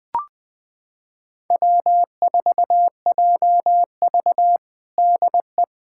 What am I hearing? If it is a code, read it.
Morse code, heard as a keyed tone, about 21 wpm, 700 Hz. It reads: W4JV DE
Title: USA Callsign DE Callsign - 4 Characters - 20wpm